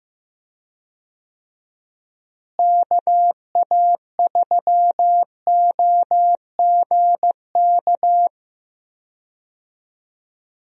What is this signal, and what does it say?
Morse code, heard as a keyed tone, about 15 wpm, 700 Hz. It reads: KA3OGK
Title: USA Call Signs - 6 Characters - Repeated 3x - 15wpm